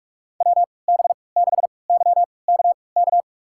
Morse code, heard as a keyed tone, about 42 wpm, spaced as for 25 wpm, 700 Hz. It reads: WB6YKK
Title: USA Call Signs - 6 Characters - 3x Character Spacing - 45wpm